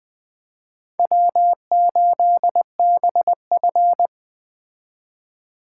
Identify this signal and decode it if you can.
Morse code, heard as a keyed tone, about 20 wpm, 700 Hz. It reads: W8BF